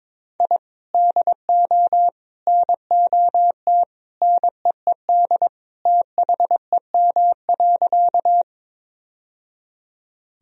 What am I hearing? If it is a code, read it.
Morse code, heard as a keyed tone, about 22 wpm, 700 Hz. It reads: I DO NOT NEED THEM.
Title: Sentences from Top 1000 Words - Review Part 1 - 22wpm